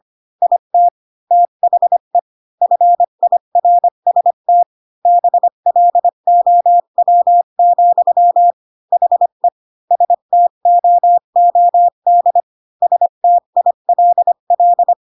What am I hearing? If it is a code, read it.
Morse code, heard as a keyed tone, about 25 wpm, 700 Hz. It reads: IT THE FIRST BLOW, HE STOOD STILL
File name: Pinocchio / Pinocchio ch1 20 25 ik1hge